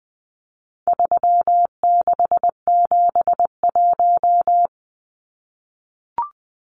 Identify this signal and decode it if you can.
Morse code, heard as a keyed tone, about 20 wpm, 700 Hz. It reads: 3671
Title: Numbers - 4-Digit 20wpm